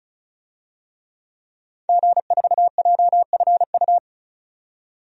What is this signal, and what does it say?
Morse code, heard as a keyed tone, about 36 wpm, 700 Hz. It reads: G4JFU
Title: DXCC-100 Call Signs - 5 Characters - 35wpm